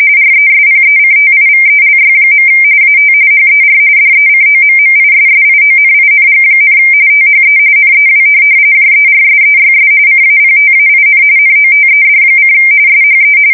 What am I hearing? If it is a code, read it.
RTTY, about 45.45 baud. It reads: RESPECTIVELY. AT HER DECOMMISSIONING, THE NEW JERSEYS CAPTAIN NOTED: SHE
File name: fsk170 / USS-New-Jersey